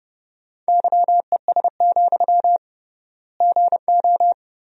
Morse code, heard as a keyed tone, about 30 wpm, 700 Hz. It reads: YES, GO
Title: 3x Word-Spacing - Sentences from Top 500 Words - 30wpm